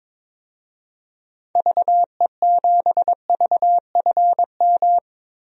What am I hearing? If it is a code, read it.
Morse code, heard as a keyed tone, about 22 wpm, 700 Hz. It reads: VE7VFM
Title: DXCC-100 Call Signs - 6 Characters - Repeated 3x - 22wpm